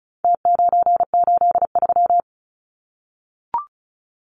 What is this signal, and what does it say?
Morse code, heard as a keyed tone, about 34 wpm, 700 Hz. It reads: T983
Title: Numbers - Contest Style - 4-Digit 35wpm